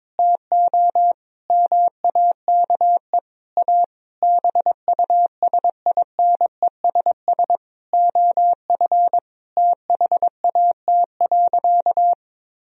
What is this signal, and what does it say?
Morse code, heard as a keyed tone, about 22 wpm, 700 Hz. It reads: TO MAKE A BUSINESS OF THAT.
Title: Sentences from Top 1000 Words - Review Part 4 - 22wpm